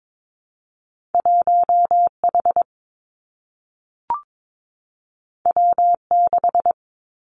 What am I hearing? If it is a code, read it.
Morse code, heard as a keyed tone, about 22 wpm, 700 Hz. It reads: 1 H W 6
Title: Two Challenging Letter-Number Combinations 22wpm